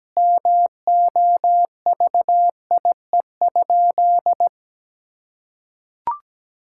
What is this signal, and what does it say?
Morse code, heard as a keyed tone, about 17 wpm, 700 Hz. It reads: MOVIE?
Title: Sentences from Top 1000 Words - Review Part 5 - 17wpm